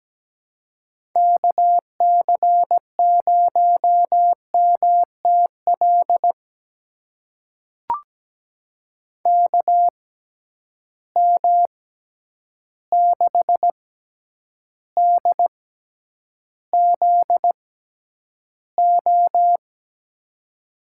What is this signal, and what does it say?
Morse code, heard as a keyed tone, about 17 wpm, spaced as for 9 wpm, 700 Hz. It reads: KC0MTL KM6DZO
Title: USA Call Signs - 6 Characters - 6x Character Spacing - 17wpm